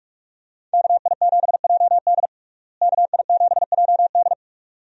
Morse code, heard as a keyed tone, about 42 wpm, 700 Hz. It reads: KI7JD KI7JD
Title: USA Call Signs - 5 Characters - Repeated 2x - 45wpm